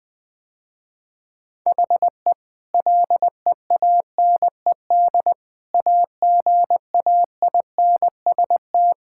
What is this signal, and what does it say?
Morse code, heard as a keyed tone, about 20 wpm, 700 Hz. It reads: HE LEANED AGAINST
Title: Sentences with 2-Word Phrases - Spoken Beginning - 20wpm